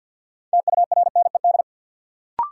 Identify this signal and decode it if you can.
Morse code, heard as a keyed tone, about 53 wpm, 700 Hz. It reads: TURNED
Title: Sets of 4 Words - New Word Drill - 50wpm